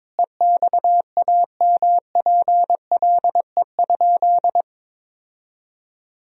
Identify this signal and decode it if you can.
Morse code, heard as a keyed tone, about 22 wpm, 700 Hz. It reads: EXAMPLE?
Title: Sentences from Top 1000 Words - Review Part 1 - 22wpm